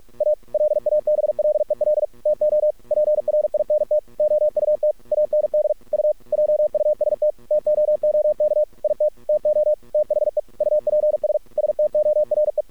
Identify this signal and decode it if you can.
Morse code, heard as a keyed tone, about 48 wpm, 610 Hz. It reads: A LABLEL TO PRINT OUT AND U JUST TOOK IT TO THE UPS STORE
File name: archived / CRSnet